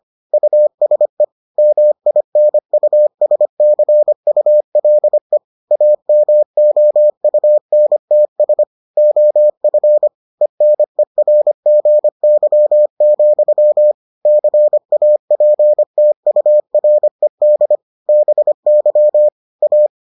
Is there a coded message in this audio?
Yes: Morse code from 0.3 to 19.9 s, heard as a keyed tone, about 25 wpm, 600 Hz. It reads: USE MINUSCULE AMOUNTS OF ENERGY, CAPTURED BY A